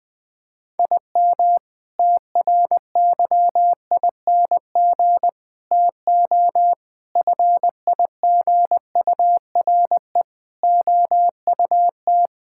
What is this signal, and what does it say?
Morse code, heard as a keyed tone, about 20 wpm, 700 Hz. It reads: IM TRYING TO FIGURE OUT
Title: Sentences with 6-Word Phrases - 20wpm